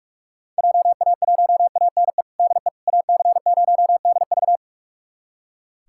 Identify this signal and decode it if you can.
Morse code, heard as a keyed tone, about 46 wpm, 700 Hz. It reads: JA1ANE DE AC0DV
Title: USA Callsign DE Callsign 45wpm